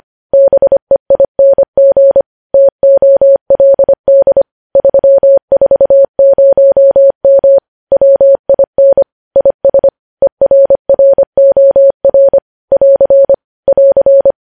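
Morse code, heard as a keyed tone, about 25 wpm, 550 Hz. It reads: BEING TOLD 340M WIN IS ERROR <AR> <AR>
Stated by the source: Morse news headlines sent at 25 words per minute.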